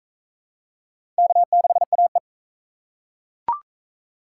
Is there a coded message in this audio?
Yes: Morse code from 1.2 to 2.2 s, heard as a keyed tone, about 40 wpm, 700 Hz. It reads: K6AE